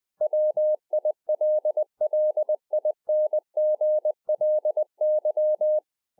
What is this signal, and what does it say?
Morse code, heard as a keyed tone, about 20 wpm, 610 Hz. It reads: WILLINGLY
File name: 2011-04-mp3 / A SMALL TRANCEIVER 3 20WPM